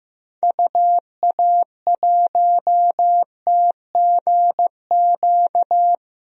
Morse code, heard as a keyed tone, about 16 wpm, 700 Hz. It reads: UA1TGQ
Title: DXCC-100 Call Signs - 6 Characters - Repeated 2x - 15wpm